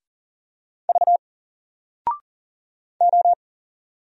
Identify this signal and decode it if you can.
Morse code, heard as a keyed tone, about 38 wpm, 700 Hz. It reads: V O
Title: Single Letter-Number 40wpm